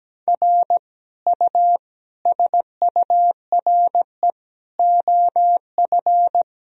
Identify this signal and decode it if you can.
Morse code, heard as a keyed tone, about 17 wpm, 700 Hz. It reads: R U SURE OF
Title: Sentences from Top 300 Words and Abbreviations 17wpm